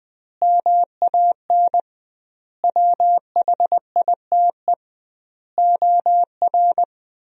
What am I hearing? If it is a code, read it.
Morse code, heard as a keyed tone, about 20 wpm, 700 Hz. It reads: MAN WHITE OR
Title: 2x Word-Spacing - Sentences from Top 400 Words - 20wpm